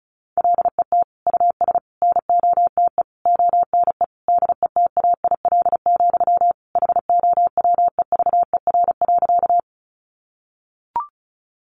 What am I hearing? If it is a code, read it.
Morse code, heard as a keyed tone, about 34 wpm, 700 Hz. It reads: LET US NOTE ONE DETAIL, HOWEVER.
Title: Sentences from Top 1500 Words 35wpm